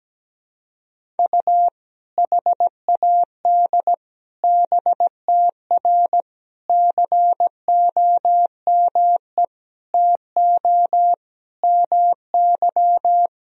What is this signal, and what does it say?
Morse code, heard as a keyed tone, about 17 wpm, 700 Hz. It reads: U HAD BTR COME TO MY
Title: Sentences from Top 400 Words and Abbreviations 17wpm